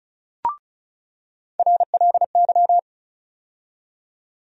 Morse code, heard as a keyed tone, about 34 wpm, 700 Hz. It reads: RLY
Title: USA Call Sign Postfixes - 3 Letters 35wpm